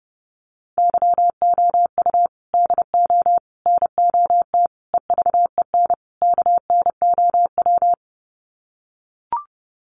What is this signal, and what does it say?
Morse code, heard as a keyed tone, about 30 wpm, 700 Hz. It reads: YOU DO NOT EVEN KNOW
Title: Sets of 5 Words Encore2 30wpm